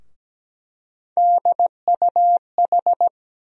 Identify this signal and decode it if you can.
Morse code, heard as a keyed tone, about 17 wpm, 700 Hz. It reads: DUH
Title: USA Call Sign Postfixes - 3 Letters 17wpm